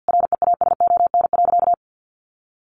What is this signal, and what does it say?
Morse code, heard as a keyed tone, about 50 wpm, 700 Hz. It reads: REASON.
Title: Sentences from Top 700 Words 50wpm